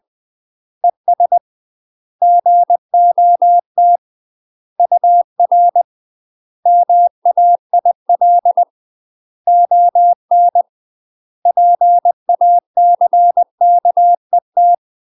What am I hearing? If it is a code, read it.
Morse code, heard as a keyed tone, about 20 wpm, 700 Hz. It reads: ES GOT UR MAIL ON PACKET
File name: UK MASTER QSOs TRAINING / 20WPM / UK MASTER QSOs 0046